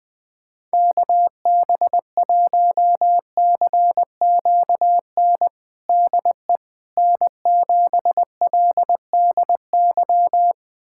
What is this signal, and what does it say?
Morse code, heard as a keyed tone, about 20 wpm, 700 Hz. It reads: KB1CQN DE N7LDY
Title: USA Callsign DE Callsign 20wpm